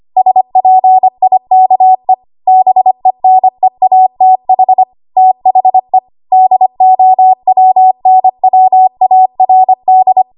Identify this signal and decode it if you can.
Morse code, heard as a keyed tone, about 25 wpm, 750 Hz. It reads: SPIKE BENEATH THE DOWNWARD
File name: Archive / 25 WPM / 25WPM